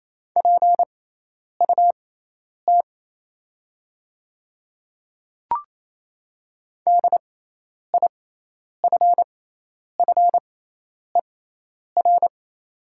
Morse code, heard as a keyed tone, about 28 wpm, spaced as for 10 wpm, 700 Hz. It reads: PUT DIFFER
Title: 6x Character Spacing - Top 200 Words - 28wpm